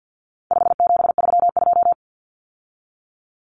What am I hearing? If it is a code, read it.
Morse code, heard as a keyed tone, about 50 wpm, 700 Hz. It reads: 5732
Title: Numbers - 4-Digit 50wpm